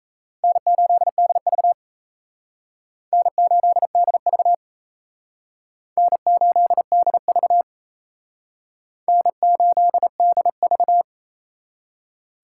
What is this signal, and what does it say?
Morse code, heard as a keyed tone, about 34 wpm, 700 Hz. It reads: N8DV N8DV N8DV N8DV
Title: Speed Racing - USA 1x2 Call Signs 28wpm